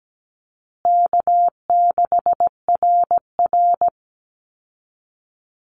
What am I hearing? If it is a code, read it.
Morse code, heard as a keyed tone, about 17 wpm, 700 Hz. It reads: K6RR
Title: Call Signs Repeated 3x - 17wpm